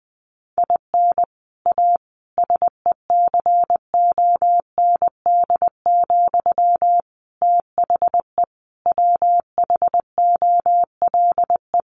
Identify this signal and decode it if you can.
Morse code, heard as a keyed tone, about 20 wpm, 700 Hz. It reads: IN A SECOND, THE WHOLE
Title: Sentences from Top 1500 Words 20wpm